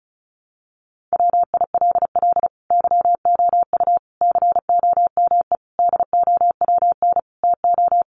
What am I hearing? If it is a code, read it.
Morse code, heard as a keyed tone, about 34 wpm, 700 Hz. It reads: WILL YOU COME DOWN TO
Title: Sentences from Top 1500 Words 35wpm